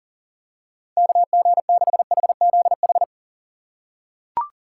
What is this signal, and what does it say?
Morse code, heard as a keyed tone, about 40 wpm, 700 Hz. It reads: KG6HZH
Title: USA Call Signs - 6 Characters - 6x Character Spacing - 40wpm